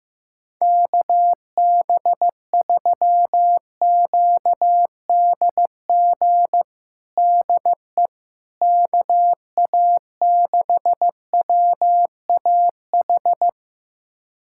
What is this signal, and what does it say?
Morse code, heard as a keyed tone, about 15 wpm, 700 Hz. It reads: KB3QDG DE KA6WAH
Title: USA Callsign DE Callsign - 6 Characters - 15wpm